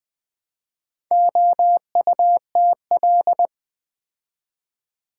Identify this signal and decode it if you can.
Morse code, heard as a keyed tone, about 20 wpm, 690 Hz. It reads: OUTL
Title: Build A Word - 2 Letters At A Time - 6-Letter Words - 20wpm